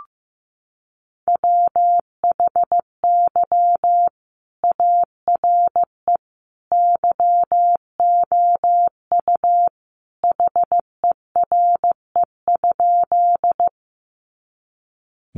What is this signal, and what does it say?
Morse code, heard as a keyed tone, about 15 wpm, 700 Hz. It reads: WHY ARE YOU HERE?
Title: Sentences from Top 200 Words 15wpm